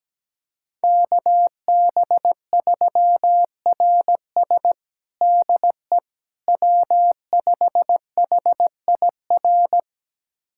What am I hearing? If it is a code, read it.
Morse code, heard as a keyed tone, about 17 wpm, 700 Hz. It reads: KB3RS DE W5HIR
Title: USA Callsign DE Callsign 17wpm